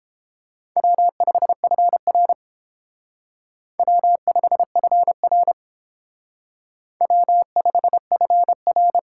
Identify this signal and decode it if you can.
Morse code, heard as a keyed tone, about 31 wpm, 700 Hz. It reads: W5FR W5FR W5FR